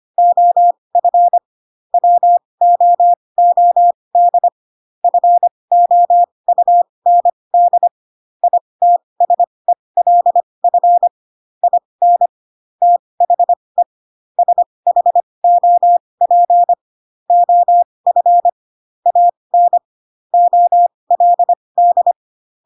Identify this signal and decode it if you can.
Morse code, heard as a keyed tone, about 25 wpm, spaced as for 20 wpm, 700 Hz. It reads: OF WOOD FOUND ITSELF IN THE SHOP OF AN OLD
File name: Pinocchio / Pinocchio ch1 15 25 ik1hge